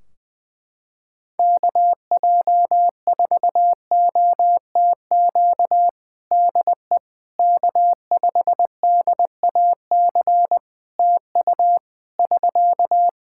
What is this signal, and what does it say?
Morse code, heard as a keyed tone, about 20 wpm, 700 Hz. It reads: KJ4OTQ DE K5DAC TU <SK>
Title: Practice QSO - Exchange 4 - Ending - 20wpm